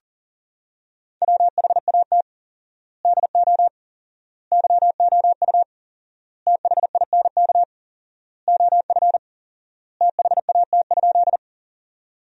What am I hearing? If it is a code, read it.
Morse code, heard as a keyed tone, about 37 wpm, 700 Hz. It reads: WHAT DO YOU THINK OF THAT?
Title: 4x Word-Spacing - Sentences from Top 200 Words - 40wpm